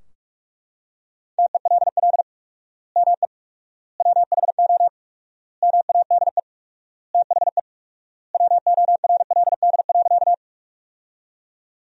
Morse code, heard as a keyed tone, about 43 wpm, 700 Hz. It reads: TELL ME WHO MADE THE WORLD.
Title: 4x Word-Spacing - Sentences from Top 200 Words - 45wpm